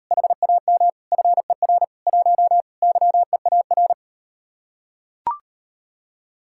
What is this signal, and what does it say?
Morse code, heard as a keyed tone, about 39 wpm, 700 Hz. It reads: HAM FER 1 YEAR